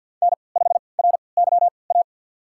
Morse code, heard as a keyed tone, about 45 wpm, spaced as for 27 wpm, 700 Hz. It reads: N5RYA
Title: USA Call Signs - 3x Character Spacing - 50wpm